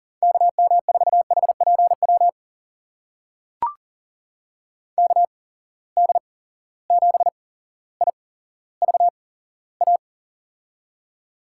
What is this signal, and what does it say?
Morse code, heard as a keyed tone, about 40 wpm, spaced as for 17 wpm, 700 Hz. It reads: KM4HPW KD7IVA
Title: USA Call Signs - 6 Characters - 8x Character Spacing - 40wpm